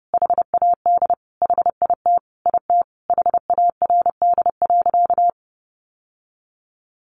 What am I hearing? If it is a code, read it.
Morse code, heard as a keyed tone, about 30 wpm, 700 Hz. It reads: HAD HIT IT HARD.
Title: Sentences from Top 1200 Words 30wpm